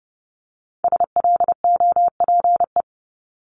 Morse code, heard as a keyed tone, about 30 wpm, 690 Hz. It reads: SLOPE
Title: Multi-letter Phonemes Repeated 3x 30wpm